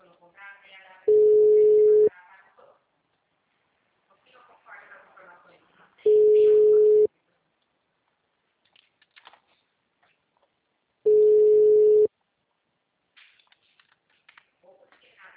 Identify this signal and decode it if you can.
Ringback tone (Europe, 425 Hz), with speech.
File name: out- -2006- - - .16881